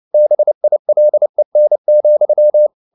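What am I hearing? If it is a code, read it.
Morse code, heard as a keyed tone, about 29 wpm, 600 Hz. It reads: BILEN,
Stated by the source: Fast (QRQ) Morse at 29 wpm, from CW med Gnister.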